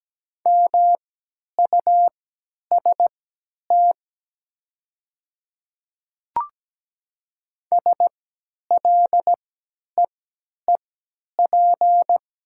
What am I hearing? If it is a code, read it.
Morse code, heard as a keyed tone, about 17 wpm, spaced as for 10 wpm, 700 Hz. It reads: MUST SLEEP
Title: 3x Character Spacing - Top 500 Words - 17wpm